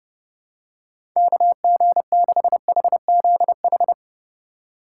Morse code, heard as a keyed tone, about 30 wpm, 700 Hz. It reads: KG6HZH
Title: USA Call Signs - 3x Character Spacing - 30wpm